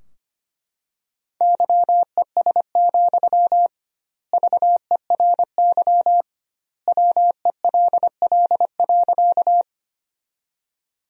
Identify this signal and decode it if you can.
Morse code, heard as a keyed tone, about 25 wpm, 700 Hz. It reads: YES, VERY WELL.
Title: 2x Word-Spacing - Sentences from Top 500 Words - 25wpm